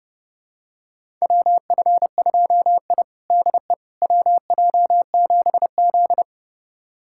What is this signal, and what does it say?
Morse code, heard as a keyed tone, about 32 wpm, 700 Hz. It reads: WF2I DE WJ7Z
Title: USA Callsign DE Callsign - 4 Characters - 30wpm